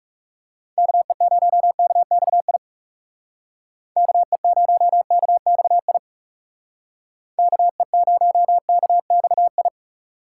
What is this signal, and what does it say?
Morse code, heard as a keyed tone, about 35 wpm, 700 Hz. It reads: KE0KXI KE0KXI KE0KXI
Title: Speed Racing - USA 2x3 Call Signs 30wpm